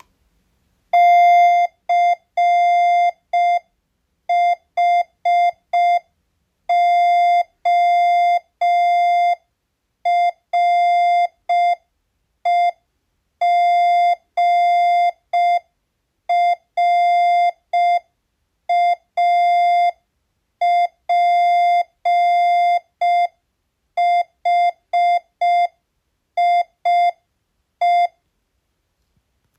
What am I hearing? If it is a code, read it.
Morse code, heard as a keyed tone, about 5 wpm, 700 Hz. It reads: CHOREGRAPHIE